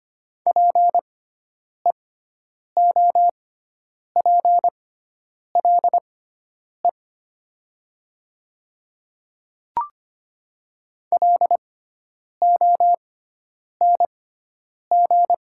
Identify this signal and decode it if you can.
Morse code, heard as a keyed tone, about 25 wpm, spaced as for 9 wpm, 700 Hz. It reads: PEOPLE LONG
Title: 6x Character Spacing - Top 300 Words - 25wpm